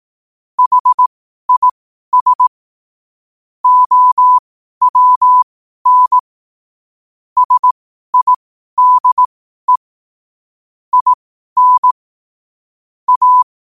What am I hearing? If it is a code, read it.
Morse code, heard as a keyed tone, about 18 wpm, spaced as for 12 wpm, 1000 Hz. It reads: HIS OWN SIDE IN A
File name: qotdfiles / 12wpm